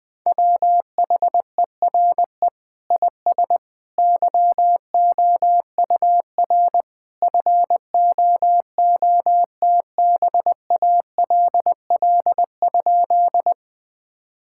Morse code, heard as a keyed tone, about 20 wpm, 700 Hz. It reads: WHERE IS YOUR FOOTBALL?
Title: Sentences from Top 1600 Words 20wpm